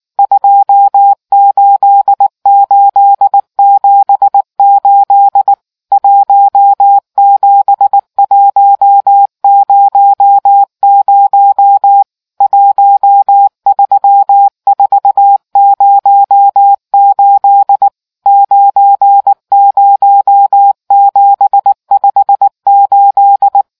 Morse code, heard as a keyed tone, about 19 wpm, 800 Hz. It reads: 28878 17100 13408 90758